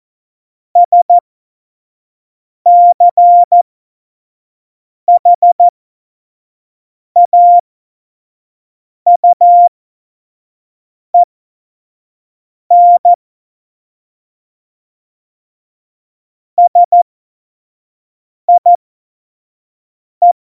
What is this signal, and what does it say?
Morse code, heard as a keyed tone, about 14 wpm, spaced as for 5 wpm, 700 Hz. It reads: SCHAUEN SIE